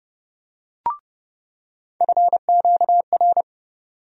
Morse code, heard as a keyed tone, about 30 wpm, 700 Hz. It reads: FQR
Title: USA Call Sign Postfixes - 3 Letters 30wpm